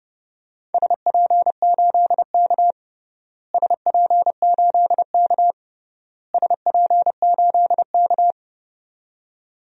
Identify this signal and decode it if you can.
Morse code, heard as a keyed tone, about 31 wpm, 700 Hz. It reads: SP8K SP8K SP8K
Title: DXCC-100 Call Signs - 4 Characters - Repeated 3x - 30wpm